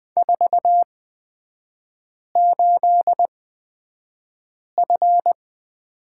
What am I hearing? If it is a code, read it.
Morse code, heard as a keyed tone, about 20 wpm, spaced as for 5 wpm, 690 Hz. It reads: 48F
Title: 229 Lesson 37e - Letter x - Three Characters 20wpm